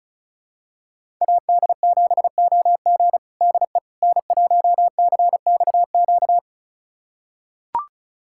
Morse code, heard as a keyed tone, about 34 wpm, 700 Hz. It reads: AD7OG DE N1CXQ
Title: USA Callsign DE Callsign - 5 Characters - 35wpm